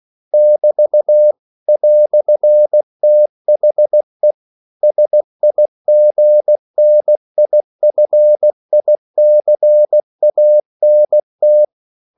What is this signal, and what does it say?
Morse code, heard as a keyed tone, about 16 wpm, 600 Hz. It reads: <BT> "THE SIGNIFICANT
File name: CW / Practice / EnglishQuotations0216